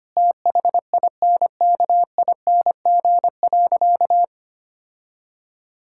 Morse code, heard as a keyed tone, about 25 wpm, 700 Hz. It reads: THINKING.
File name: practice / 25wpm / Difficult Sentences with Hints - 25wpm